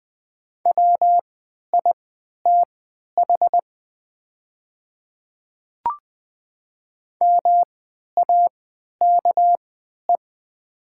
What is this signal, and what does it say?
Morse code, heard as a keyed tone, about 20 wpm, spaced as for 11 wpm, 700 Hz. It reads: WITH MAKE
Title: 3x Character Spacing - Top 100 Words - 20wpm